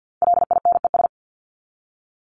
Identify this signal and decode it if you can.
Morse code, heard as a keyed tone, about 48 wpm, 700 Hz. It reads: LINES
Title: ES:S Suffix - Top 100 Words with Base Words 50wpm